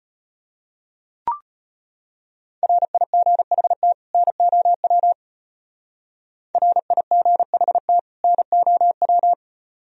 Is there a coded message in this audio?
Yes: Morse code from 2.6 to 9.3 s, heard as a keyed tone, about 31 wpm, 700 Hz. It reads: RIGHT NOW RIGHT NOW